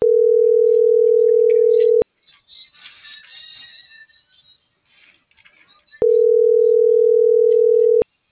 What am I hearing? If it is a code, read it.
Ringback tone (North America), 440 + 480 Hz.